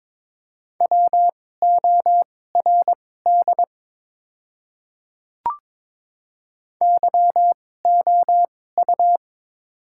Morse code, heard as a keyed tone, about 22 wpm, spaced as for 16 wpm, 700 Hz. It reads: WORD YOU
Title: 2x Character Spacing - Top 100 Words - 22wpm